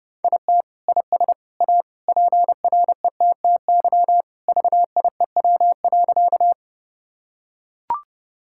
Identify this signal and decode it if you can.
Morse code, heard as a keyed tone, about 32 wpm, 700 Hz. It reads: IT IS A PRETTY VIEW.
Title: Sentences from Top 1000 Words - Review Part 6 - 30wpm